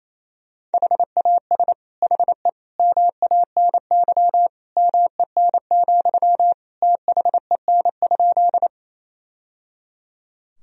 Morse code, heard as a keyed tone, about 28 wpm, 700 Hz. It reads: HAS HE MANY MEN, THEN?
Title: Sentences from Top 1000 Words - Review Part 1 - 28wpm